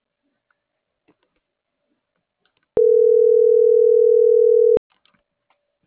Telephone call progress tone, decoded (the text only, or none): ringback tone